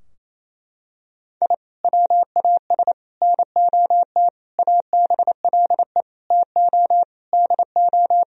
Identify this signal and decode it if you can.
Morse code, heard as a keyed tone, about 28 wpm, 700 Hz. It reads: I WAS NOT ABLE TO DO
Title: Sentences from Top 1000 Words - Review Part 2 - 28wpm